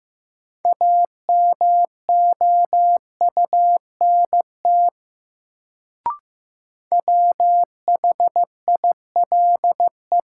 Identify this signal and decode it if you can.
Morse code, heard as a keyed tone, about 15 wpm, 700 Hz. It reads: AMOUNT WHILE
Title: Sets of 3 Words - New Word Drill - 15wpm